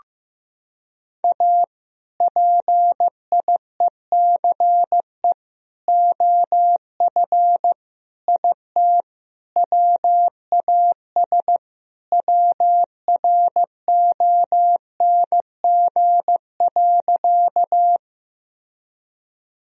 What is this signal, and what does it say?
Morse code, heard as a keyed tone, about 15 wpm, 700 Hz. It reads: A PIECE OF IT WAS WRONG.
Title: Sentences from Top 1000 Words - Review Part 2 - 15wpm